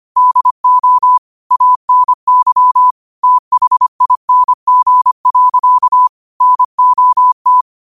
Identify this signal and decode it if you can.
Morse code, heard as a keyed tone, about 25 wpm, 1000 Hz. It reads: DO ANYTHING. NOT
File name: 25wpm